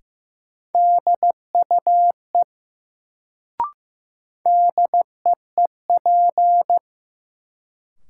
Morse code, heard as a keyed tone, about 15 wpm, 700 Hz. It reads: DUE DEEP
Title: Sets of 5 Words Encore - New Word Drill - 15wpm